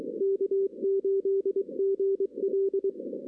Morse code, heard as a keyed tone, about 28 wpm, spaced as for 25 wpm, 410 Hz. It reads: K8GL
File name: 2016 arrl dx cw / 2016-ARRL-DX-CW-K5TR- - 0-9